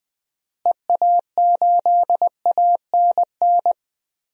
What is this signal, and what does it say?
Morse code, heard as a keyed tone, about 21 wpm, 700 Hz. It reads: EA8ANN
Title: DXCC-100 Call Signs - 6 Characters - Repeated 2x - 20wpm